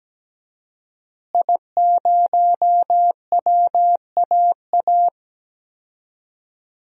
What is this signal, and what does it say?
Morse code, heard as a keyed tone, about 17 wpm, 700 Hz. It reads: I0WAA
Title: DXCC-100 Call Signs - 5 Characters - Repeated 2x - 17wpm